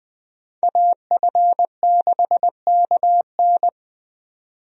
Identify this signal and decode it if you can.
Morse code, heard as a keyed tone, about 20 wpm, 700 Hz. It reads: AF6KN
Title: USA Call Signs - 5 Characters - 20wpm